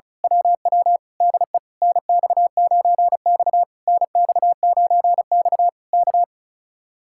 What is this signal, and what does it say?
Morse code, heard as a keyed tone, about 32 wpm, 700 Hz. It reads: WW DE NX9X NX9X K
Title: Speed-Racing Common QSO Phrases Encore - 35wpm